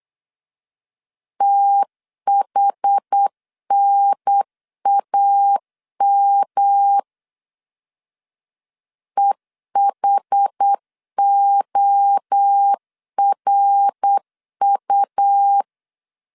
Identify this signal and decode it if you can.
Morse code, heard as a keyed tone, about 8 wpm, 800 Hz. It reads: THNAM EHORU
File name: CW / DET / 1 3 1500C S C 8wpm 4214